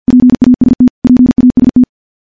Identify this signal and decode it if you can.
Morse code, heard as a keyed tone, about 52 wpm, 250 Hz. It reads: PAST PAST
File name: morse / morse-training-50WPM